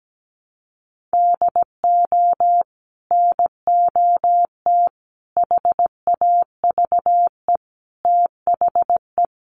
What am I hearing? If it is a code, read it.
Morse code, heard as a keyed tone, about 17 wpm, 700 Hz. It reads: DO NOT HAVE THE
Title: Sets of 4 Words 17wpm